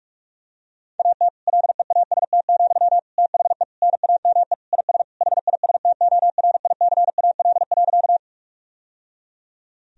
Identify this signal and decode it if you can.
Morse code, heard as a keyed tone, about 43 wpm, 700 Hz. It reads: AT LEAST, THE NAME IS HISTORICAL.
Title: Sentences from Top 1000 Words - Review Part 4 - 45wpm